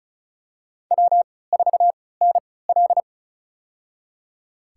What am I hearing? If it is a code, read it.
Morse code, heard as a keyed tone, about 34 wpm, spaced as for 20 wpm, 700 Hz. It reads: W4NL
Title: USA Call Signs - 4 Characters - 3x Character Spacing - 35wpm